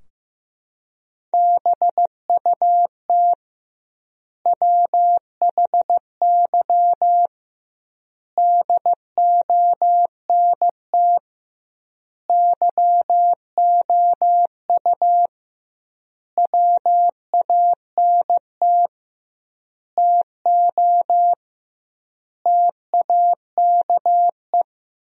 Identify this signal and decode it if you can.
Morse code, heard as a keyed tone, about 15 wpm, 700 Hz. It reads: BUT WHY DONT YOU WANT TO TAKE
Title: 2x Word-Spacing - Sentences from Top 300 Words - 15wpm